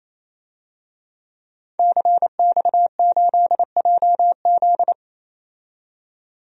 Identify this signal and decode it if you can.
Morse code, heard as a keyed tone, about 28 wpm, 700 Hz. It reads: CX8JZ
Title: DXCC-100 Call Signs - 5 Characters - 28wpm